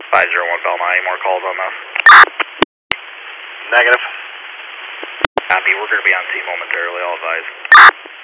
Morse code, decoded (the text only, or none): none